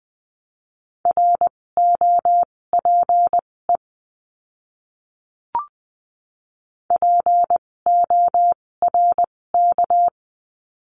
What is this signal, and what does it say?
Morse code, heard as a keyed tone, about 20 wpm, spaced as for 16 wpm, 700 Hz. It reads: ROPE PORK
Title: 256 Lesson 42a - Words 12-20wpm